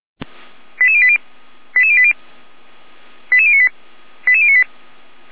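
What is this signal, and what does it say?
Five-tone selective call (ZVEI), heard as quick runs of five tones, tones of about 70 ms. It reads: 88080 88080 88098 88098
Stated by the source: Five-tone ZVEI selective call received by SDR, address 88571.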